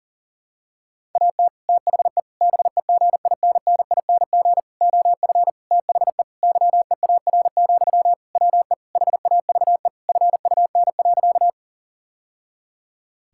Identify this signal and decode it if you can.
Morse code, heard as a keyed tone, about 40 wpm, 700 Hz. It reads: AT THE BEGINNING OF THE YEAR, WE HAVE FUN.
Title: Sentences from Top 1000 Words - Review Part 5 - 40wpm